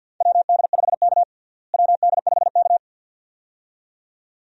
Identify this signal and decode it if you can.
Morse code, heard as a keyed tone, about 50 wpm, 700 Hz. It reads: WD5K WD5K
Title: USA Call Signs - 4 Characters - Repeated 2x - 50wpm